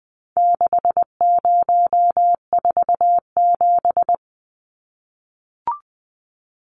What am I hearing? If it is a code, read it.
Morse code, heard as a keyed tone, about 20 wpm, 700 Hz. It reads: 6047
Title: Numbers - 4-Digit 20wpm